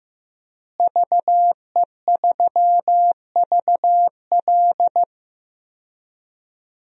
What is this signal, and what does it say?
Morse code, heard as a keyed tone, about 15 wpm, 700 Hz. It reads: VE3VL
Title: DXCC-100 Call Signs - 5 Characters - Repeated 2x - 15wpm